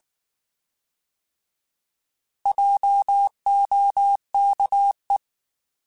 Morse code, heard as a keyed tone, about 19 wpm, 800 Hz. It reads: JOKE
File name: Alphabetical / Alphabetical mp3s / Js @18